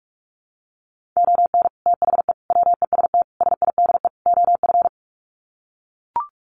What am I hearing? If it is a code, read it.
Morse code, heard as a keyed tone, about 44 wpm, 700 Hz. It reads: ON THE WEST SIDE OF
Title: Sets of 5 Words Encore2 45wpm